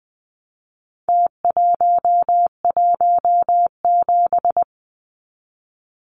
Morse code, heard as a keyed tone, about 20 wpm, 700 Hz. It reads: T117
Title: Numbers - Contest Style - 4-Digit 20wpm